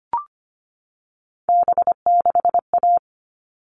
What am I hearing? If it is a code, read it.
Morse code, heard as a keyed tone, about 25 wpm, 700 Hz. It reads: B6A
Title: All Cut Numbers - 3-Digit 25wpm